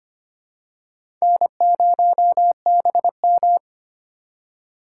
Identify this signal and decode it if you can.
Morse code, heard as a keyed tone, about 25 wpm, 700 Hz. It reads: N0BM
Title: USA Call Signs - 4 Characters - 3x Character Spacing - 25wpm